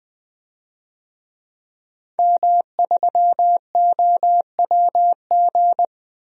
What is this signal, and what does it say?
Morse code, heard as a keyed tone, about 20 wpm, 700 Hz. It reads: M3OWG
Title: DXCC-100 Call Signs - 5 Characters - 20wpm